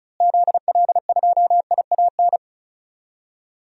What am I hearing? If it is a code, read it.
Morse code, heard as a keyed tone, about 34 wpm, 700 Hz. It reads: ZL2IAN
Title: DXCC-100 Call Signs - 6 Characters - Repeated 3x - 35wpm